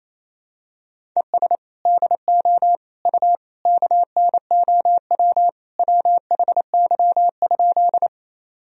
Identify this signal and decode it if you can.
Morse code, heard as a keyed tone, about 28 wpm, 700 Hz. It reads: ES DO U KNOW WHY?
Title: Sentences from Top 200 Words and Abbreviations 28wpm